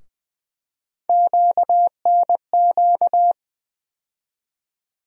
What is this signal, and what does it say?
Morse code, heard as a keyed tone, about 20 wpm, 700 Hz. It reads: QNQ
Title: USA Call Sign Postfixes - 3 Letters 20wpm